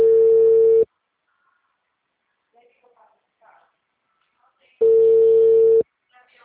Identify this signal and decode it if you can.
Ringback tone (China, 450 Hz), with speech.